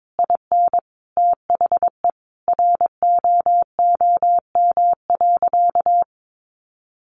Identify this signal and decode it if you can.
Morse code, heard as a keyed tone, about 22 wpm, 700 Hz. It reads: IN THE ROOM.
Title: A Sound Of Thunder 22wpm